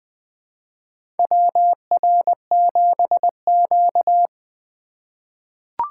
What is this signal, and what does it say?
Morse code, heard as a keyed tone, about 20 wpm, 700 Hz. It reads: WR7Q
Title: USA Call Signs - 4 Characters - 2x Character Spacing - 20wpm